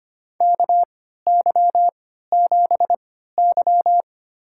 Morse code, heard as a keyed tone, about 25 wpm, spaced as for 14 wpm, 700 Hz. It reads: KY7Y
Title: USA Call Signs - 3x Character Spacing - 25wpm